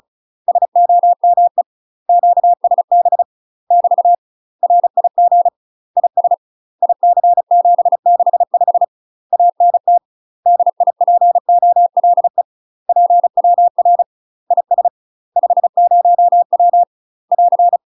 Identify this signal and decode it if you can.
Morse code, heard as a keyed tone, about 36 wpm, 700 Hz. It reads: SOME QSB <BT> RIG IS IC765 ANT DIPOLE PWR IS 50W <AR>
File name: UK MASTER QSOs TRAINING / 35WPM / UK MASTER QSOs 0033